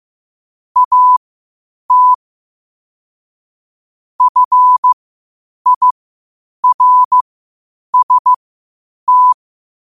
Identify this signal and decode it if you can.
Morse code, heard as a keyed tone, about 15 wpm, spaced as for 8 wpm, 1000 Hz. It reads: AT FIRST
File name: qotdfiles / 8wpm